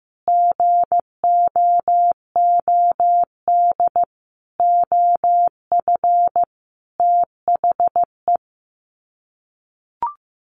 Morse code, heard as a keyed tone, about 15 wpm, 700 Hz. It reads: GOOD OF THE
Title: Sets of 5 Words Encore2 15wpm